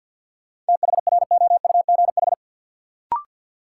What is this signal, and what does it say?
Morse code, heard as a keyed tone, about 53 wpm, 700 Hz. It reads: THROUGH
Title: Sentences from Top 200 Words - New Word Drill - 50wpm